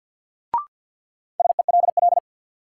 Morse code, heard as a keyed tone, about 47 wpm, 700 Hz. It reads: SELL
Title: Sentences from Top 1100 Words - New Word Drill Alt - 50wpm